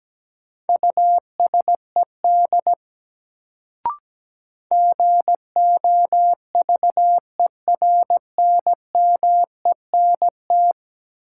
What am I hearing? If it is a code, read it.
Morse code, heard as a keyed tone, about 17 wpm, 700 Hz. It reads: USED GOVERNMENT
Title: Sentences from Top 600 Words - New Word Drill - 17wpm